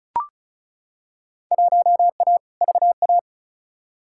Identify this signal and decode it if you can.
Morse code, heard as a keyed tone, about 36 wpm, 700 Hz. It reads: 1A VA
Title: Field Day Exchanges Encore - 35wpm